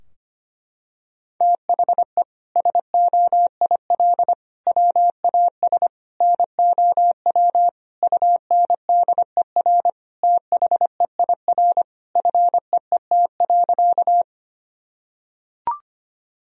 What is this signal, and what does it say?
Morse code, heard as a keyed tone, about 25 wpm, 700 Hz. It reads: THE SOIL WAS NOW UNDER THEIR FEET.
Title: Sentences from Top 1000 Words - Review Part 4 - 25wpm